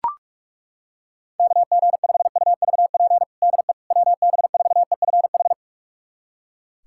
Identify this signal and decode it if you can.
Morse code, heard as a keyed tone, about 43 wpm, 700 Hz. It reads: KG5UVP DE WB4EFH
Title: USA Callsign DE Callsign 45wpm